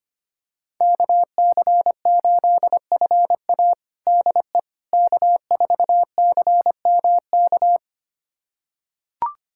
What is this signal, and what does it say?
Morse code, heard as a keyed tone, about 25 wpm, 700 Hz. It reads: KC8FA DE K4CMK
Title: USA Callsign DE Callsign - 5 Characters - 25wpm